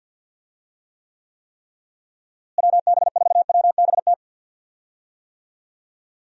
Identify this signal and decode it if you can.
Morse code, heard as a keyed tone, about 50 wpm, 700 Hz. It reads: WB4WBT
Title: USA Call Signs - 6 Characters - Repeated 3x - 50wpm